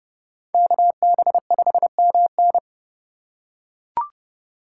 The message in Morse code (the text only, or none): KB5MN